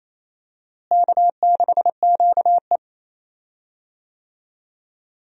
Morse code, heard as a keyed tone, about 28 wpm, 700 Hz. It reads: K6QE
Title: USA Call Signs - 4 Characters - Repeated 3x - 28wpm